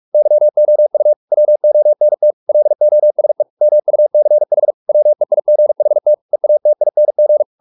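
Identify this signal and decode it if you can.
Morse code, heard as a keyed tone, about 44 wpm, 600 Hz. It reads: YOU WONT LOSE MUCH WEIGHT EATING
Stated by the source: Morse practice at 45 wpm, the novel Never by Ken Follett in English, from CW med Gnister.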